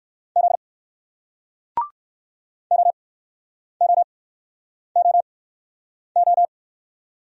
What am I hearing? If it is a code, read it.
Morse code, heard as a keyed tone, about 60 wpm, spaced as for 9 wpm, 700 Hz. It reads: Z OOOO